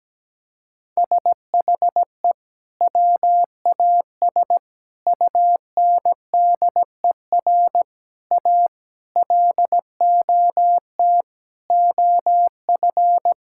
Morse code, heard as a keyed tone, about 17 wpm, 700 Hz. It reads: SHE WAS UNDER A LOT OF
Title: Sentences from Top 1000 Words - Review Part 6 - 17wpm